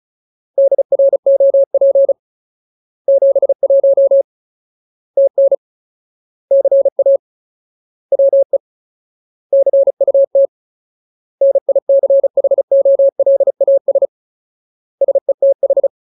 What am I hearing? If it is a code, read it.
Morse code, heard as a keyed tone, about 37 wpm, 550 Hz. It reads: DROP 71 TN CA WE CUT NICHOLAS SETH
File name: Ultimate Word List / 35WPM Ultimate